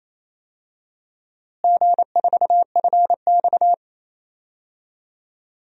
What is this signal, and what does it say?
Morse code, heard as a keyed tone, about 28 wpm, 700 Hz. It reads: G4FX
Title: DXCC-100 Call Signs - 4 Characters - Repeated 2x - 28wpm